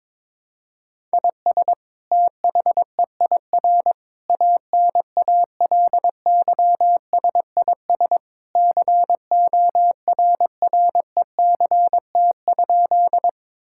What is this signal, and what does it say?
Morse code, heard as a keyed tone, about 22 wpm, 700 Hz. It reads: IS THEIR ANALYSIS CORRECT?
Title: Sentences from Top 1000 Words - Review Part 5 - 22wpm